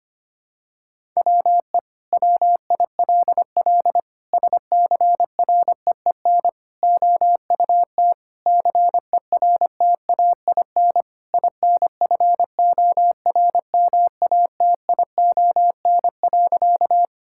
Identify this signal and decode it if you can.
Morse code, heard as a keyed tone, about 25 wpm, 700 Hz. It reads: WE WILL SCREEN OUT CERTAIN INFORMATION.
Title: Sentences from Top 1000 Words - Review Part 6 - 25wpm